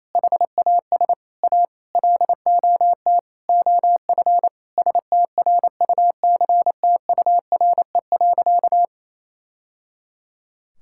Morse code, heard as a keyed tone, about 28 wpm, 700 Hz. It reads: HAS A LOT OF STRUCTURE.
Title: Sentences from Top 1000 Words - Review Part 6 - 28wpm